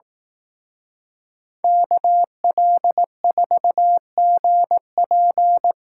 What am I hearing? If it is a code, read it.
Morse code, heard as a keyed tone, about 18 wpm, 700 Hz. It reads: KL4GP
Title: Speed Racing - USA 2x2 Call Signs 15wpm